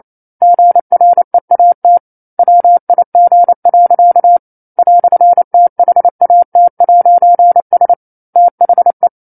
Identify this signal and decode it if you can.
Morse code, heard as a keyed tone, about 28 wpm, 700 Hz. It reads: GREAT WIG. "THAT'S THE
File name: 09-Chapter-09 28 WPM